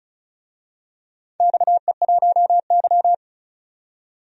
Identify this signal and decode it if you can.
Morse code, heard as a keyed tone, about 36 wpm, 700 Hz. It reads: XE1Y
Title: DXCC-100 Call Signs - 4 Characters - 35wpm